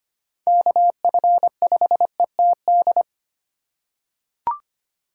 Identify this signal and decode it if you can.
Morse code, heard as a keyed tone, about 25 wpm, 700 Hz. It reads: KF5ETD
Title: K1USN SST QSO Practice - 25wpm